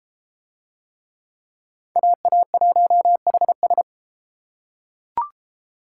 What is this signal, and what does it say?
Morse code, heard as a keyed tone, about 33 wpm, 700 Hz. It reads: AA1HS